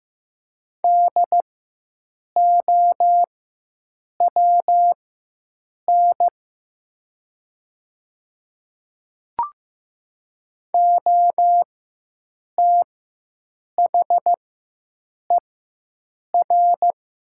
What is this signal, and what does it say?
Morse code, heard as a keyed tone, about 15 wpm, spaced as for 7 wpm, 700 Hz. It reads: DOWN OTHER
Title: 4x Character Spacing - Top 200 Words - 15wpm